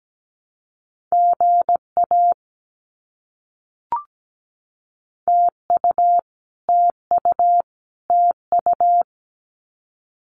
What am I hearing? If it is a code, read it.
Morse code, heard as a keyed tone, about 17 wpm, 700 Hz. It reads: GA TU TU TU
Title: Abbreviations Repeated 3x - 17wpm